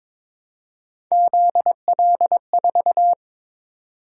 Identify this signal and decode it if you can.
Morse code, heard as a keyed tone, about 22 wpm, 700 Hz. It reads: ZL4
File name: practice / 22wpm / Three Letter-Number Combinations - 22wpm